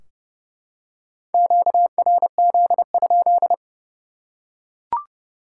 Morse code, heard as a keyed tone, about 30 wpm, 700 Hz. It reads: QRZ?
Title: Top 180 QSO Elements 30wpm